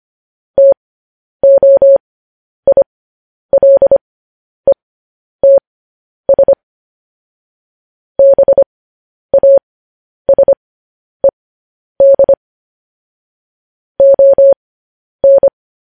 Morse code, heard as a keyed tone, about 25 wpm, spaced as for 10 wpm, 550 Hz. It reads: TOILETS BASED ON
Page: News Headlines in Morse Code at 10 WPM